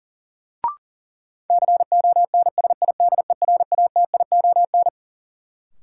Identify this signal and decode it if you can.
Morse code, heard as a keyed tone, about 40 wpm, 700 Hz. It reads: CONSIDERATION
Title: Sentences from Top 1500 Words - New Word Drill Alt - 40wpm